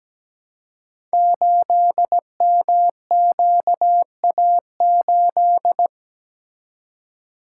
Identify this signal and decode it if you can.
Morse code, heard as a keyed tone, about 17 wpm, 700 Hz. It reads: 8MQA8
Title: Code Groups - 5 Letter:Numbers - 17wpm